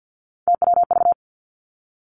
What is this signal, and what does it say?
Morse code, heard as a keyed tone, about 50 wpm, 700 Hz. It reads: TWV
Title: Three Challenging Letter-Number Combinations 50wpm